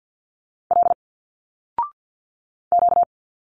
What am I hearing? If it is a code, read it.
Morse code, heard as a keyed tone, about 50 wpm, 700 Hz. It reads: L Q
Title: Single Letters 50wpm